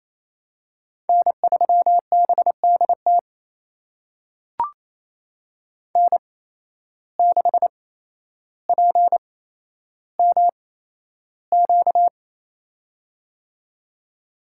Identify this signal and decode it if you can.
Morse code, heard as a keyed tone, about 28 wpm, spaced as for 12 wpm, 700 Hz. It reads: N3BDT N6PMQ
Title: USA Call Signs - 5 Characters - 8x Character Spacing - 28wpm